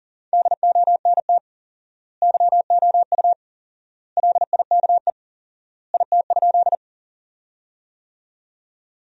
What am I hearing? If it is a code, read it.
Morse code, heard as a keyed tone, about 40 wpm, 700 Hz. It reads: DONT YOU LIKE IT?
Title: 4x Word-Spacing - Sentences from Top 300 Words - 40wpm